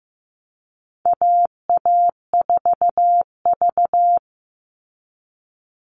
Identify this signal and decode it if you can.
Morse code, heard as a keyed tone, about 15 wpm, 700 Hz. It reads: AA4V
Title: All Cut Numbers - 4-Digit 15wpm